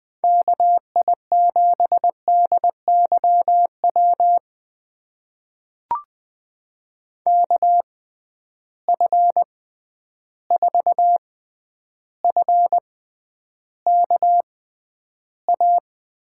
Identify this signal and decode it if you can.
Morse code, heard as a keyed tone, about 20 wpm, spaced as for 10 wpm, 700 Hz. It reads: KI7DYW KF4FKA
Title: USA Call Signs - 6 Characters - 6x Character Spacing - 20wpm